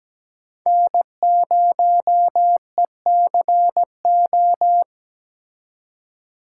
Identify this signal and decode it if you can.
Morse code, heard as a keyed tone, about 20 wpm, 700 Hz. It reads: N0ECO
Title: Speed Racing - USA 1x3 Call Signs 17wpm